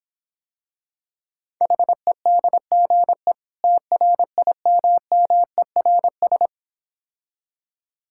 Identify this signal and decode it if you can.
Morse code, heard as a keyed tone, about 26 wpm, 700 Hz. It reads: HEDGE TRIMMERS